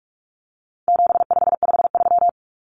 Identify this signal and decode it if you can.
Morse code, heard as a keyed tone, about 44 wpm, 700 Hz. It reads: 7553
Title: Numbers - 4-Digit 45wpm